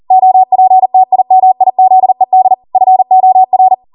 Morse code, heard as a keyed tone, about 40 wpm, 750 Hz. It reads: OPTIMIZED FOR
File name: Morse / 40WPM